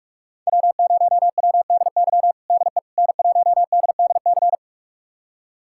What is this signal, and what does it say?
Morse code, heard as a keyed tone, about 46 wpm, 700 Hz. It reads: W0WDY DE N1DDC
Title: USA Callsign DE Callsign - 5 Characters - 45wpm